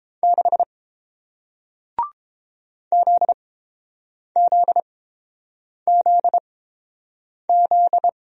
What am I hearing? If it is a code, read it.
Morse code, heard as a keyed tone, about 30 wpm, spaced as for 7 wpm, 700 Hz. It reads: 6 ZZZZ